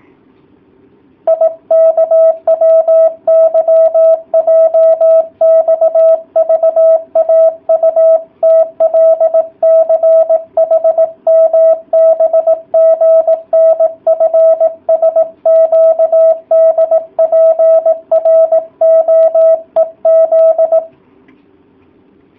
Morse code, heard as a keyed tone, about 18 wpm, 650 Hz. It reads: IKWYJXVAUTLCHMBGNFSQDPROEZ